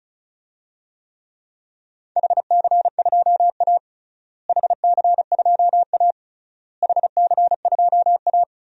Morse code, heard as a keyed tone, about 36 wpm, 700 Hz. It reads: HC2A HC2A HC2A
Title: DXCC-100 Call Signs - 4 Characters - Repeated 3x - 35wpm